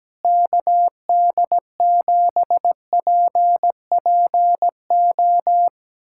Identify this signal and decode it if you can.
Morse code, heard as a keyed tone, about 17 wpm, 700 Hz. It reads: KD7PPO
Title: USA Callsign DE Callsign 17wpm